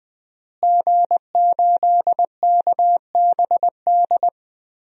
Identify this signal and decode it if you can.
Morse code, heard as a keyed tone, about 20 wpm, 700 Hz. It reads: G8KBD
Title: DXCC-100 Call Signs - 5 Characters - Repeated 2x - 20wpm